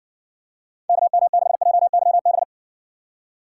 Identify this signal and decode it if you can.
Morse code, heard as a keyed tone, about 60 wpm, 700 Hz. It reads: KM6JYB